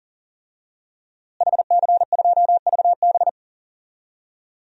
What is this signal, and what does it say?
Morse code, heard as a keyed tone, about 38 wpm, 700 Hz. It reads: HC2VB
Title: DXCC-100 Call Signs - 5 Characters - Repeated 2x - 40wpm